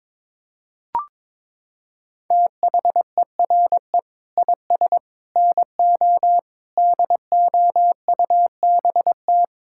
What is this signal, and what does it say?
Morse code, heard as a keyed tone, about 22 wpm, 700 Hz. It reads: THERE IS NO DOUBT
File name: Sets of 9 Words - 22wpm